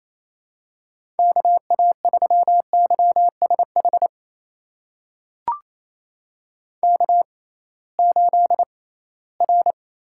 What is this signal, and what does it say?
Morse code, heard as a keyed tone, about 28 wpm, spaced as for 18 wpm, 700 Hz. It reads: KA3YSH K8R
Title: USA Call Signs - 6x Character Spacing - 28wpm